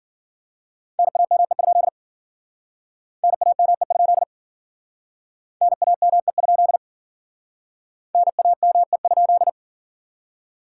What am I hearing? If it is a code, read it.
Morse code, heard as a keyed tone, about 54 wpm, 700 Hz. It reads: NAME? NAME? NAME? NAME?